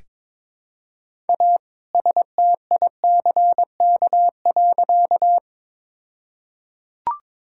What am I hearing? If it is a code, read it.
Morse code, heard as a keyed tone, about 22 wpm, 700 Hz. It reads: A STICK.
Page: morse-code-ninja-practice-22wpm